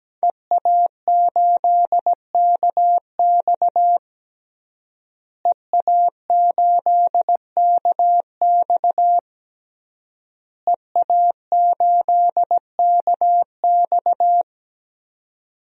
Morse code, heard as a keyed tone, about 17 wpm, 700 Hz. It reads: EA8KX EA8KX EA8KX
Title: DXCC-100 Call Signs - 5 Characters - Repeated 3x - 17wpm